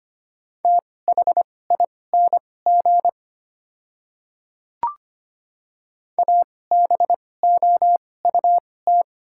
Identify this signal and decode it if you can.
Morse code, heard as a keyed tone, about 25 wpm, spaced as for 18 wpm, 690 Hz. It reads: THING ABOUT
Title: 2x Character Spacing - Top 100 Words - 25wpm